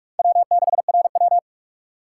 Morse code, heard as a keyed tone, about 46 wpm, 700 Hz. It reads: W6RW
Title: USA Call Signs - 4 Characters - 4x Character Spacing - 45wpm